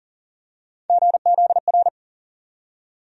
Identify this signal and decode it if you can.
Morse code, heard as a keyed tone, about 37 wpm, 700 Hz. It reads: GZR
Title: USA Call Sign Postfixes - 3 Letters 40wpm